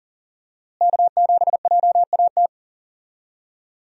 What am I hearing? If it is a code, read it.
Morse code, heard as a keyed tone, about 40 wpm, 700 Hz. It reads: K7JAT
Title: USA Call Signs - 5 Characters - Repeated 2x - 40wpm